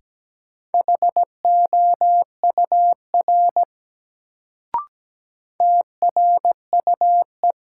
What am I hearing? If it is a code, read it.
Morse code, heard as a keyed tone, about 17 wpm, 700 Hz. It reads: HOUR TRUE
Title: Sentences from Top 400 Words - New Word Drill - 17wpm